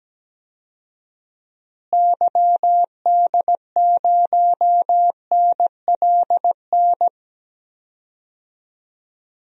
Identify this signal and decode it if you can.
Morse code, heard as a keyed tone, about 17 wpm, 700 Hz. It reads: YD0NLN
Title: DXCC-100 Call Signs - 6 Characters - Repeated 3x - 17wpm